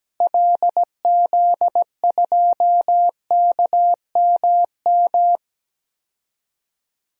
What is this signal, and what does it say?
Morse code, heard as a keyed tone, about 17 wpm, 700 Hz. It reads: LZ2KMM
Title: DXCC-100 Call Signs - 6 Characters - Repeated 2x - 17wpm